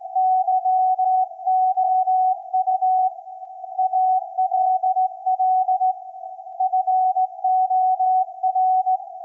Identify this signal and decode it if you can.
Morse code, heard as a keyed tone, about 17 wpm, 750 Hz. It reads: YOU ALL FOR
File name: iz7slz / VLF / SAQ 24-12-2024